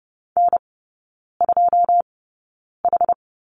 Morse code, heard as a keyed tone, about 30 wpm, spaced as for 9 wpm, 700 Hz. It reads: N2H
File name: 081 Lesson 14e - Number 2 - Three Characters 30wpm